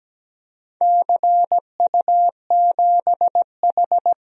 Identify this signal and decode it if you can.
Morse code, heard as a keyed tone, about 17 wpm, 700 Hz. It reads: CU7H
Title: DXCC-100 Call Signs - 4 Characters - Repeated 3x - 17wpm